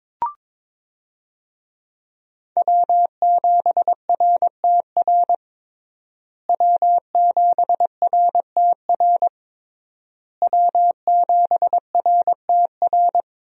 Morse code, heard as a keyed tone, about 22 wpm, 700 Hz. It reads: W7RTR W7RTR W7RTR
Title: USA Call Signs - 5 Characters - Repeated 3x - 22wpm